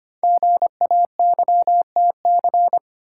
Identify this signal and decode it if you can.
Morse code, heard as a keyed tone, about 25 wpm, 700 Hz. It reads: GAYTC
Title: Code Groups - 5 Letters - 25wpm